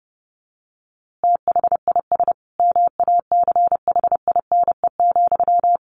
Morse code, heard as a keyed tone, about 30 wpm, 700 Hz. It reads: THIS MACHINE,
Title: A Sound Of Thunder 30wpm